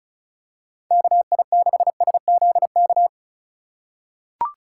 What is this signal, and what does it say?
Morse code, heard as a keyed tone, about 36 wpm, 700 Hz. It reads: KI6SZK
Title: USA Call Signs - 6 Characters - 2x Character Spacing - 35wpm